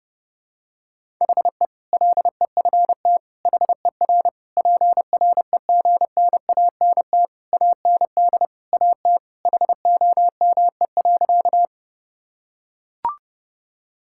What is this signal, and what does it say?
Morse code, heard as a keyed tone, about 32 wpm, 700 Hz. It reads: HE LEFT HER PREGNANT AND AT HOME.
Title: Sentences from Top 1000 Words - Review Part 6 - 30wpm